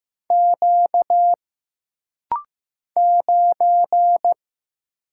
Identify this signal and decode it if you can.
Morse code, heard as a keyed tone, about 15 wpm, 700 Hz. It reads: Q 9
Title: Single Letter-Number Alt - Rapid-Fire 15wpm